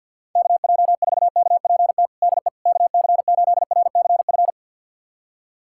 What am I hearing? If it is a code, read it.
Morse code, heard as a keyed tone, about 50 wpm, 700 Hz. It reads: KJ4KPT DE KC8RCF
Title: USA Callsign DE Callsign - 6 Characters - 50wpm